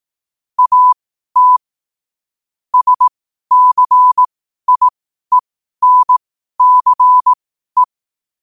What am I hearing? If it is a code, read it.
Morse code, heard as a keyed tone, about 18 wpm, spaced as for 12 wpm, 1000 Hz. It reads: AT SCIENCE
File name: qotdfiles / 12wpm